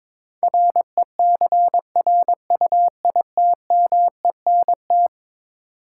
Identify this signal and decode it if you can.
Morse code, heard as a keyed tone, about 21 wpm, 700 Hz. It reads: RECRUITMENT
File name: practice / 22wpm / Eleven Letter Words - 22wpm